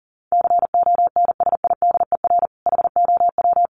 Morse code, heard as a keyed tone, about 40 wpm, 700 Hz. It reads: CONSIDER HOW
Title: Sentences from Top 1000 Words 40wpm